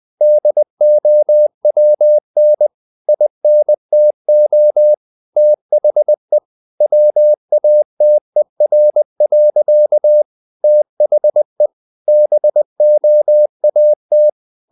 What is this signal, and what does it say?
Morse code, heard as a keyed tone, about 20 wpm, 600 Hz. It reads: DOWN INTO THE WATER. THE BOAT
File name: the old man and the sea / the old man and the sea part 1 20wpm